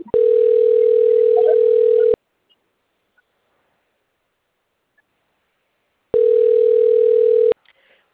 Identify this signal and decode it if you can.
Ringback tone (North America), 440 + 480 Hz.